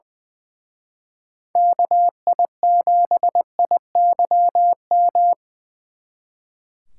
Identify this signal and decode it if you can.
Morse code, heard as a keyed tone, about 24 wpm, 700 Hz. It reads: KI7IYM
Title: Speed Racing - USA Call Signs 20wpm